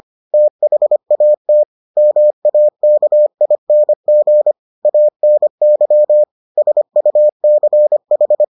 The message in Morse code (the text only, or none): THAT MAKING ANY SUCH